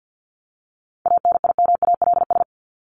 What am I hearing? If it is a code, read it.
Morse code, heard as a keyed tone, about 50 wpm, 700 Hz. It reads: ANIMALS
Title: ES:S Suffix - Top 100 Words with Base Words 50wpm